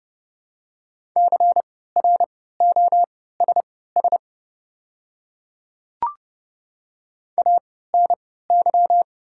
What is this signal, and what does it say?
Morse code, heard as a keyed tone, about 29 wpm, spaced as for 17 wpm, 690 Hz. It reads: CROSS ANY
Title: 3x Character Spacing - Top 300 Words - 30wpm